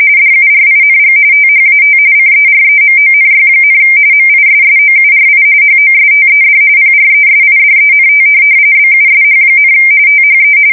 RTTY, about 45.45 baud. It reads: DURING THAT CEREMONY, MISSOURI HOSTED PRESIDENT BUSH, THE
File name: fsk170 / USS-Missouri